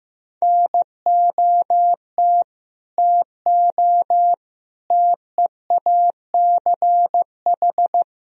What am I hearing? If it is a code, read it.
Morse code, heard as a keyed tone, about 15 wpm, 700 Hz. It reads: NOT TO TEACH
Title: Sentences from Top 1000 Words - Review Part 4 - 15wpm